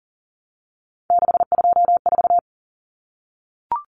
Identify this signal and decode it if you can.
Morse code, heard as a keyed tone, about 37 wpm, 700 Hz. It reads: 624
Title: Numbers - 3-Digit 40wpm